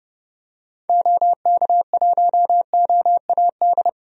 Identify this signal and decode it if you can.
Morse code, heard as a keyed tone, about 30 wpm, 700 Hz. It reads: OK1OAD
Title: DXCC-100 Call Signs - 6 Characters - 30wpm